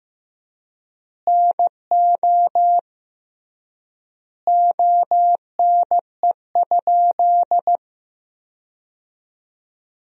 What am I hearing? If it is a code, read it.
Morse code, heard as a keyed tone, about 16 wpm, 700 Hz. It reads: NO ONE?
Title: 3x Word-Spacing - Sentences from Top 300 Words - 15wpm